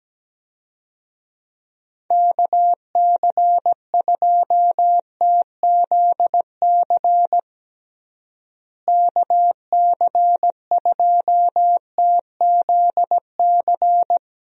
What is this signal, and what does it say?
Morse code, heard as a keyed tone, about 17 wpm, 700 Hz. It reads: KC2TZC KC2TZC
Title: USA Call Signs - 6 Characters - Repeated 2x - 17wpm